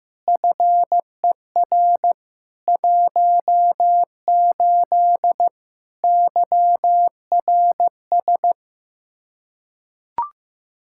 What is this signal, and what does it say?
Morse code, heard as a keyed tone, about 15 wpm, 700 Hz. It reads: FER 18 YRS
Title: Practice QSO - Exchange 3 - More Info - 15wpm